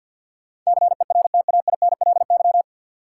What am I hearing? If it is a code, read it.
Morse code, heard as a keyed tone, about 48 wpm, 700 Hz. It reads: CERTAINLY
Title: Sentences from Top 900 Words - New Word Drill - 50wpm